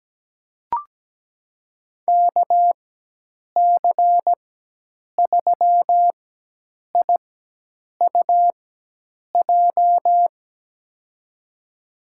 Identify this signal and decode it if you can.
Morse code, heard as a keyed tone, about 17 wpm, spaced as for 8 wpm, 700 Hz. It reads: KC3IUJ
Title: USA Call Signs - 4x Character Spacing - 17wpm